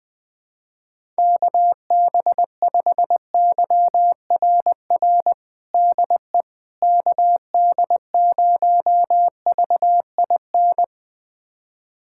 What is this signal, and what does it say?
Morse code, heard as a keyed tone, about 20 wpm, 690 Hz. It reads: KB5YRR DE KD0VIN
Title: USA Callsign DE Callsign 20wpm